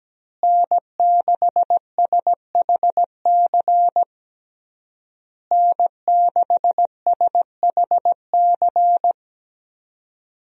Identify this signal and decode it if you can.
Morse code, heard as a keyed tone, about 17 wpm, 700 Hz. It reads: N6SHC N6SHC
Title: USA Call Signs - 5 Characters - Repeated 3x - 17wpm